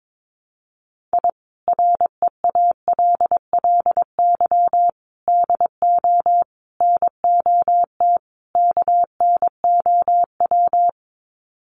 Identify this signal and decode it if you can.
Morse code, heard as a keyed tone, about 22 wpm, 700 Hz. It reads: I REALLY DO NOT KNOW
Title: Sets of 5 Words Encore 22wpm